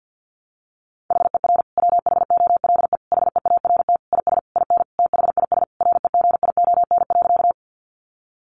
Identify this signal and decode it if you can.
Morse code, heard as a keyed tone, about 50 wpm, 700 Hz. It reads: HER WHOLE HEART IS IN THIS REGION.
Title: Sentences from Top 1500 Words 50wpm